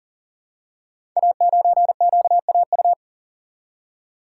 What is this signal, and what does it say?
Morse code, heard as a keyed tone, about 39 wpm, 690 Hz. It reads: A9QAU
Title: Code Groups - 5 Letter:Numbers - 40wpm